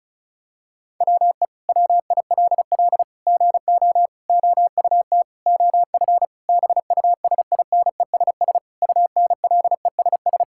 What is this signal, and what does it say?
Morse code, heard as a keyed tone, about 34 wpm, 700 Hz. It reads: WE WILL GO OUT OF BUSINESS UNLESS
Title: Sentences from Top 1000 Words - Review Part 6 - 35wpm